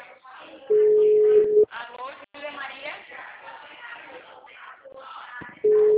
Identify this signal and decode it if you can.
Ringback tone (Europe, 425 Hz), with speech.